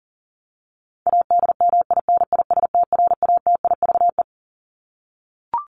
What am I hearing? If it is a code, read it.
Morse code, heard as a keyed tone, about 40 wpm, 700 Hz. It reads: ADMINISTRATIVE
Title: Top 1501-1750 Words 40wpm